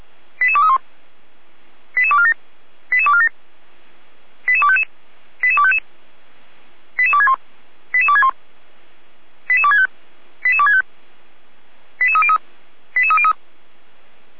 Five-tone selective call (ZVEI), heard as quick runs of five tones, tones of about 70 ms. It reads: 88231 88247 88247 88255 88255 88261 88261 88265 88265 88303 88303